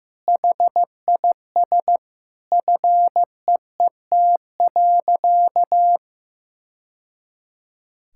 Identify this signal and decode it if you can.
Morse code, heard as a keyed tone, about 15 wpm, 700 Hz. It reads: HIS FEET.
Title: Sentences from Top 1000 Words - Review Part 5 - 15wpm